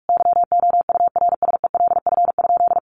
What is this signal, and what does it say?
Morse code, heard as a keyed tone, about 44 wpm, 700 Hz. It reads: YOURSELF?
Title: Sentences from Top 1200 Words 45wpm